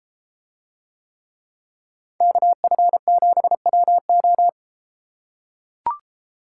Morse code, heard as a keyed tone, about 32 wpm, 700 Hz. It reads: KF7WO